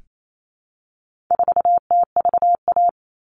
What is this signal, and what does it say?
Morse code, heard as a keyed tone, about 28 wpm, 700 Hz. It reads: 4TVA
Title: All Cut Numbers - 4-Digit 28wpm